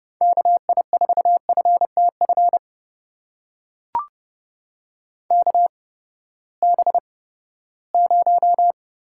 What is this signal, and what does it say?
Morse code, heard as a keyed tone, about 30 wpm, spaced as for 17 wpm, 700 Hz. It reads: KI4FTF KB0
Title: USA Call Signs - 6 Characters - 8x Character Spacing - 30wpm